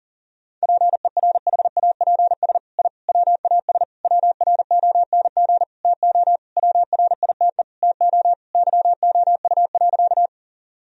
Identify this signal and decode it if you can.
Morse code, heard as a keyed tone, about 40 wpm, 700 Hz. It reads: PERHAPS I WAS WRONG TO WRITE TO YOU.
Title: Sentences from Top 1000 Words - Review Part 1 - 40wpm